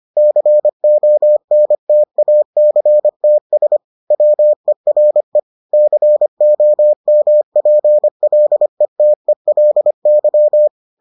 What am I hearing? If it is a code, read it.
Morse code, heard as a keyed tone, about 25 wpm, 600 Hz. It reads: CONTACTS WERE COMPLETELY